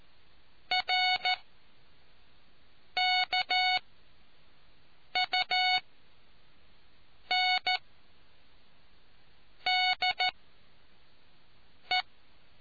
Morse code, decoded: RKUNDE

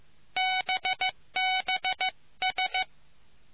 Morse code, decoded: BBS

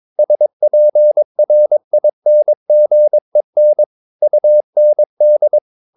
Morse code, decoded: SPRINGEN UND

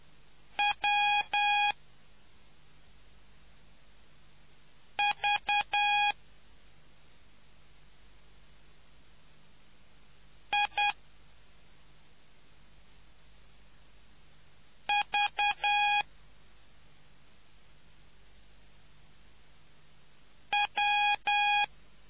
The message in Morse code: W V I V W

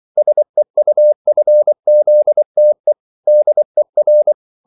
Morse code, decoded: SEUFZTE DER

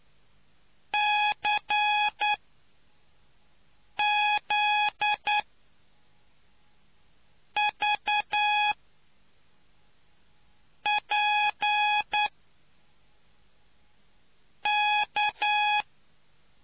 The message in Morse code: CZVPK

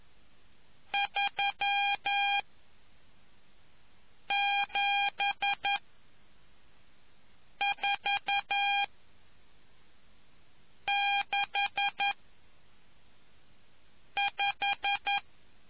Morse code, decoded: 37465

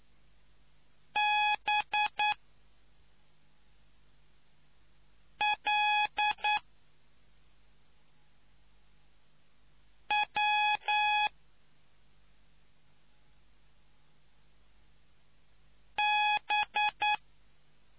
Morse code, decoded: BLW B

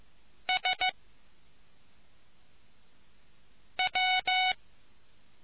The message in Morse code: S W

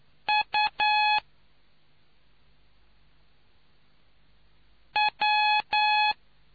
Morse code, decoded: U W

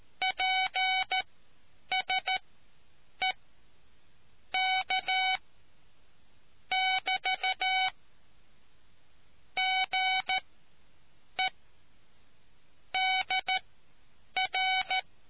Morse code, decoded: PSE K <BT> GE DR

